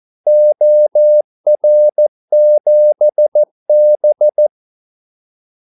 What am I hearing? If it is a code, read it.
Morse code, heard as a keyed tone, about 14 wpm, 600 Hz. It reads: OR7B
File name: Callsigns-14wpm